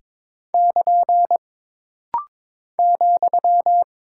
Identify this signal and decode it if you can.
Morse code, heard as a keyed tone, about 22 wpm, 700 Hz. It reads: <KN> ,
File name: practice / 22wpm / Single Punctuation-Prosign - Rapid-Fire - 22wpm